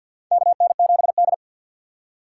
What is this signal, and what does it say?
Morse code, heard as a keyed tone, about 49 wpm, 700 Hz. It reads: KN7D
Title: K1USN SST QSO Practice - 50wpm